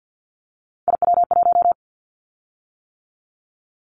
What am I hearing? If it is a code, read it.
Morse code, heard as a keyed tone, about 50 wpm, 700 Hz. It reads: IW1
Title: Callsign Prefixes for Top 25 DXCC - 50wpm